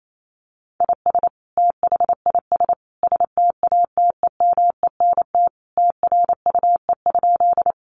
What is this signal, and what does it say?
Morse code, heard as a keyed tone, about 28 wpm, 700 Hz. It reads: IS THIS STATEMENT TRUE?
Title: Sentences from Top 1200 Words 28wpm